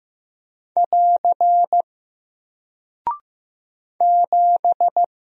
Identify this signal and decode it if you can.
Morse code, heard as a keyed tone, about 15 wpm, 700 Hz. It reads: <AR> 7
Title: Single Character (Letter:Number:Punctuation:Prosign) Alt 15wpm